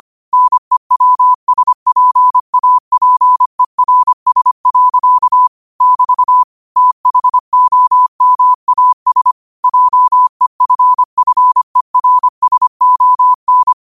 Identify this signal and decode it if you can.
Morse code, heard as a keyed tone, about 25 wpm, 1000 Hz. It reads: NEWSPAPERS. <BT> THOMAS JEFFERSON
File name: qotdfiles / 25wpm